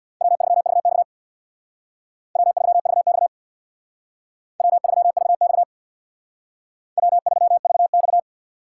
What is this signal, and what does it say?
Morse code, heard as a keyed tone, about 59 wpm, 700 Hz. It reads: W3VX W3VX W3VX W3VX
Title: Speed Racing - USA Call Signs 50wpm